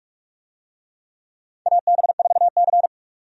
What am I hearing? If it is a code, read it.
Morse code, heard as a keyed tone, about 44 wpm, 700 Hz. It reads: AB4C
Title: USA Call Signs - 4 Characters - 45wpm